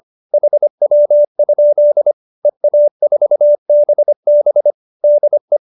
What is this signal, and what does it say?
Morse code, heard as a keyed tone, about 25 wpm, 570 Hz. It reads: HW? EA4BB DE